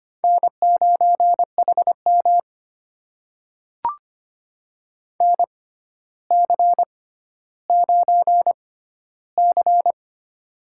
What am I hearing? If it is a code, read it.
Morse code, heard as a keyed tone, about 25 wpm, spaced as for 13 wpm, 700 Hz. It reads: N9HM NC9C
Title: USA Call Signs - 6x Character Spacing - 25wpm